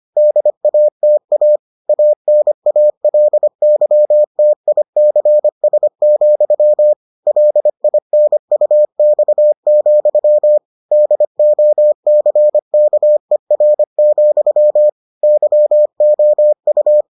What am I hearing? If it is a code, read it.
Morse code, heard as a keyed tone, about 25 wpm, 600 Hz. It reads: DATA ANALYTICS, LINUX, DOCKER, YOU